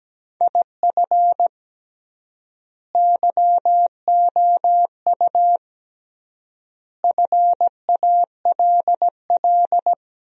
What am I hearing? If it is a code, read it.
Morse code, heard as a keyed tone, about 17 wpm, 700 Hz. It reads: IF YOU FALL
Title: 3x Word-Spacing - Sentences from Top 500 Words - 17wpm